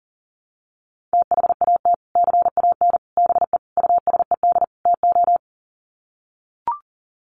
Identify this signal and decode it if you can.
Morse code, heard as a keyed tone, about 40 wpm, 700 Hz. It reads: THAT CAN BE USED TO
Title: Sets of 5 Words Encore 40wpm